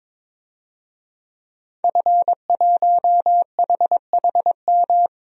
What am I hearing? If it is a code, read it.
Morse code, heard as a keyed tone, about 22 wpm, 700 Hz. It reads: F1HHM
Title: DXCC-100 Call Signs - 5 Characters - Repeated 2x - 22wpm